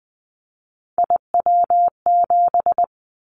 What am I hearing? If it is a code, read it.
Morse code, heard as a keyed tone, about 20 wpm, 700 Hz. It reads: IW7
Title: Callsign Prefixes for Top 25 DXCC - 20wpm